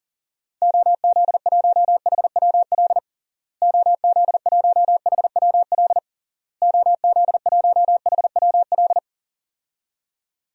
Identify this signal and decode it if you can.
Morse code, heard as a keyed tone, about 40 wpm, 700 Hz. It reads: OZ1HWL OZ1HWL OZ1HWL
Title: DXCC-100 Call Signs - 6 Characters - Repeated 3x - 40wpm